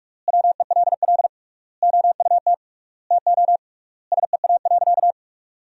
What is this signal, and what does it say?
Morse code, heard as a keyed tone, about 42 wpm, 700 Hz. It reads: WELL OUT TO SEA.
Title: 3x Word-Spacing - Sentences from Top 400 Words - 45wpm